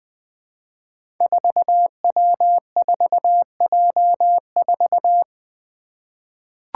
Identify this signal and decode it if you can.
Morse code, heard as a keyed tone, about 20 wpm, 700 Hz. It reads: 4W4J4
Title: Five Challenging Letter-Number Combinations 20wpm